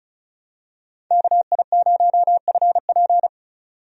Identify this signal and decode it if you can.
Morse code, heard as a keyed tone, about 36 wpm, 700 Hz. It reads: KI0FP
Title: USA Call Signs - 5 Characters - 35wpm